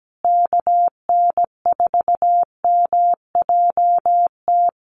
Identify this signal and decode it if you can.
Morse code, heard as a keyed tone, about 17 wpm, 700 Hz. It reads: KN4MJT
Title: USA Call Signs Repeated 3x 17wpm